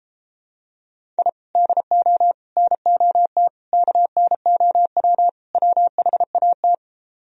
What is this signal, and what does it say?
Morse code, heard as a keyed tone, about 33 wpm, 700 Hz. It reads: I DO NOT KNOW WHAT